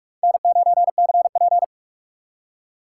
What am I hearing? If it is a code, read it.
Morse code, heard as a keyed tone, about 56 wpm, 700 Hz. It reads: N9CP
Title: Speed Racing - USA 1x2 Call Signs 45wpm